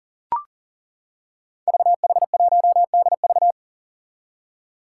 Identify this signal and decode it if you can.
Morse code, heard as a keyed tone, about 40 wpm, 700 Hz. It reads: VH1DV
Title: Five Challenging Letter-Number Combinations 40wpm